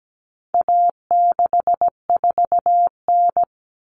Morse code, heard as a keyed tone, about 17 wpm, 700 Hz. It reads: A64N
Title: Common Cut Numbers - 4-Digit 17wpm